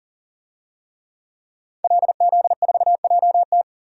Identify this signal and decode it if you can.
Morse code, heard as a keyed tone, about 40 wpm, 700 Hz. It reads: LZ4JT
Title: DXCC-100 Call Signs - 5 Characters - 40wpm